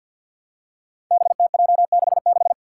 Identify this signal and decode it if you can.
Morse code, heard as a keyed tone, about 52 wpm, 700 Hz. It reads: BTJ66
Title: Five Challenging Letter-Number Combinations 50wpm